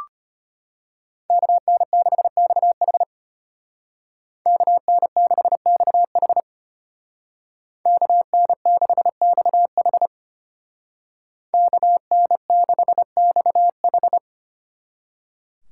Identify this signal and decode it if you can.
Morse code, heard as a keyed tone, about 31 wpm, 700 Hz. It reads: KN6XH KN6XH KN6XH KN6XH
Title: Speed Racing - USA Call Signs 25wpm